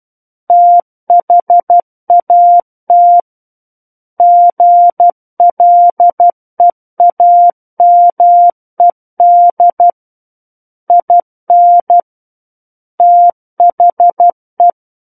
Morse code, heard as a keyed tone, about 12 wpm, 700 Hz. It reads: THAT GLEAMED IN THE